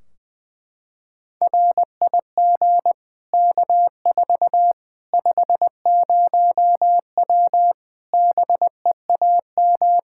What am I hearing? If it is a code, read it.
Morse code, heard as a keyed tone, about 21 wpm, 700 Hz. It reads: RIG K4 50W BEAM
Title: Practice QSO - Exchange 2 - Info - 20wpm